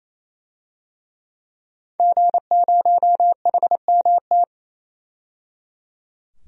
Morse code, heard as a keyed tone, about 27 wpm, 700 Hz. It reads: G0HMT
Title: DXCC-100 Call Signs - 5 Characters - 28wpm